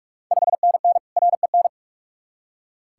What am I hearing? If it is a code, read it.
Morse code, heard as a keyed tone, about 46 wpm, 700 Hz. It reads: 5NN REN
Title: Ontario QSO Party Exchange 45wpm